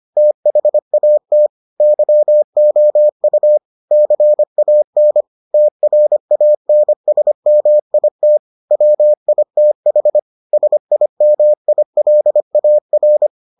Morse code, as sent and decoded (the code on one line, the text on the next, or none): - .... .- - / -.-- --- ..- / -.-. .- -. / - .-. .- -. ... -- .. - / .-- .. - .... / ... .. -- .. .-.. .- .-.
THAT YOU CAN TRANSMIT WITH SIMILAR